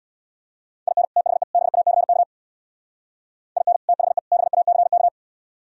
Morse code, heard as a keyed tone, about 62 wpm, 700 Hz. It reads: IN THE BACK IN THE BACK
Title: Speed-Racing - Sets of 3 Words 50wpm